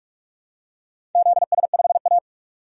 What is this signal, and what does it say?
Morse code, heard as a keyed tone, about 43 wpm, 700 Hz. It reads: ZS5A
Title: DXCC-100 Call Signs - 4 Characters - 45wpm